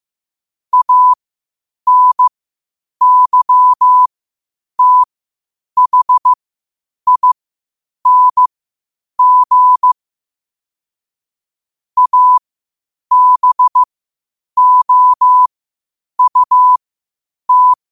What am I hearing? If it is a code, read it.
Morse code, heard as a keyed tone, about 15 wpm, spaced as for 8 wpm, 1000 Hz. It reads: ANYTHING ABOUT
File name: qotdfiles / 8wpm